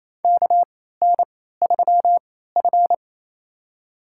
Morse code, heard as a keyed tone, about 28 wpm, spaced as for 16 wpm, 700 Hz. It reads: KN3F
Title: USA Call Signs - 4 Characters - 3x Character Spacing - 28wpm